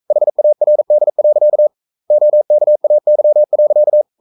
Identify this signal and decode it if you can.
Morse code, heard as a keyed tone, about 42 wpm, 600 Hz. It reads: HARD. OKAY.
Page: Romanen Never af Ken Follet på engelsk.